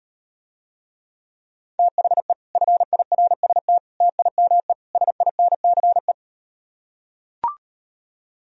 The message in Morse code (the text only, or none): THE FIRST TIME SINCE